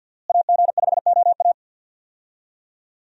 Morse code, heard as a keyed tone, about 52 wpm, 700 Hz. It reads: AG5OA
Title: USA Call Signs - 5 Characters - Repeated 2x - 50wpm